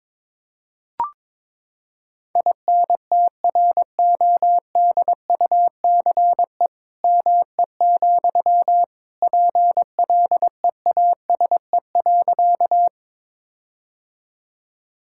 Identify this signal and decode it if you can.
Morse code, heard as a keyed tone, about 22 wpm, 700 Hz. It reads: INTRODUCE ME, PLEASE.
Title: Sentences from Top 1600 Words 22wpm